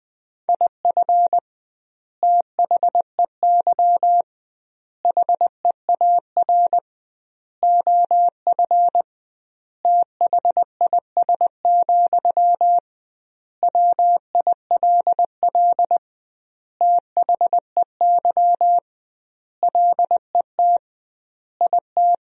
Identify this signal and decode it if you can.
Morse code, heard as a keyed tone, about 20 wpm, 700 Hz. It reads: IF THEY HEAR OF THIS, WILL THEY LET IT
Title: 2x Word-Spacing - Sentences from Top 400 Words - 20wpm